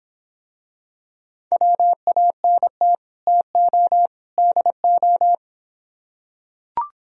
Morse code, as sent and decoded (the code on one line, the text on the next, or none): .-- .- -. - / - --- / -.. ---
WANT TO DO